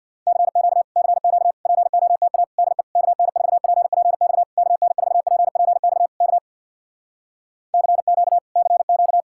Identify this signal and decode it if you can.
Morse code, heard as a keyed tone, about 64 wpm, 700 Hz. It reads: CQ CQ POTA DE KN4PPX KN4PPX K CQ CQ